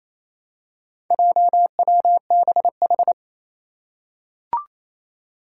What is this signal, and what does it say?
Morse code, heard as a keyed tone, about 27 wpm, 700 Hz. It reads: JWBH
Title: Four Challenging Letter-Number Combinations 28wpm